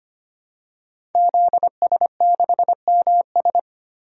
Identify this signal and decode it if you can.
Morse code, heard as a keyed tone, about 25 wpm, 700 Hz. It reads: ZS6MS
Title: DXCC-100 Call Signs - 5 Characters - 25wpm